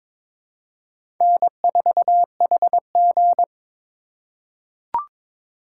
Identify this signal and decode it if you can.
Morse code, heard as a keyed tone, about 21 wpm, 700 Hz. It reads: N4HG
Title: USA Call Signs - 4 Characters - 4x Character Spacing - 22wpm